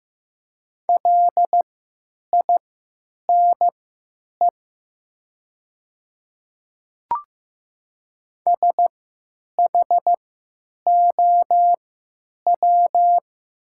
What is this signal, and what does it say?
Morse code, heard as a keyed tone, about 15 wpm, spaced as for 9 wpm, 700 Hz. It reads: LINE SHOW
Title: 3x Character Spacing - Top 200 Words - 15wpm